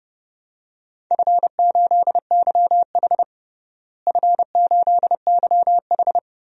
Morse code, heard as a keyed tone, about 32 wpm, 700 Hz. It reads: F8YH F8YH
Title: DXCC-100 Call Signs - 4 Characters - Repeated 3x - 30wpm